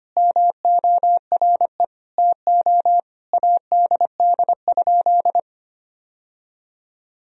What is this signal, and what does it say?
Morse code, heard as a keyed tone, about 25 wpm, 700 Hz. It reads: MORE TO ADD?
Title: Sentences from Top 1000 Words - Review Part 1 - 25wpm